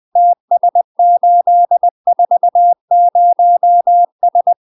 Morse code, heard as a keyed tone, about 20 wpm, 700 Hz. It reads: TS840S
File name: UK MASTER QSOs TRAINING / 20WPM / UK MASTER QSOs 0029